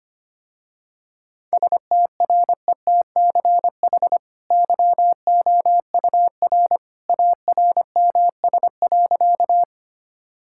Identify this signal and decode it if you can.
Morse code, heard as a keyed tone, about 25 wpm, 700 Hz. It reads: STRETCH YOUR ARMS.
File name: Easy Sentences with Hints - 25wpm